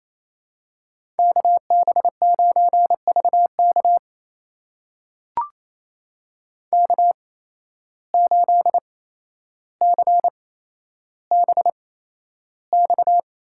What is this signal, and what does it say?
Morse code, heard as a keyed tone, about 28 wpm, spaced as for 12 wpm, 700 Hz. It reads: KB9VK K8CBX
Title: USA Call Signs - 5 Characters - 8x Character Spacing - 28wpm